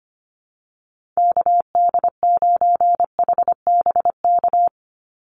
Morse code, heard as a keyed tone, about 25 wpm, 700 Hz. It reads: KD9HBK
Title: USA Call Signs 25wpm